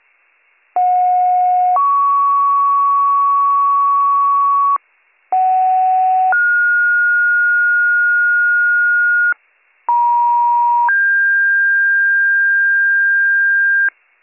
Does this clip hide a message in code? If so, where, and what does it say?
Yes: two-tone page from 0.8 to 13.9 s, heard as a one-second tone, then a three-second tone, about 730 then 1120 Hz, about 750 then 1430 Hz, about 950 then 1600 Hz, with noise.